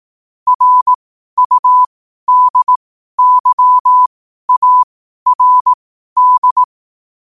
Morse code, heard as a keyed tone, about 18 wpm, spaced as for 12 wpm, 1000 Hz. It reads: RUDYARD
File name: qotdfiles / 12wpm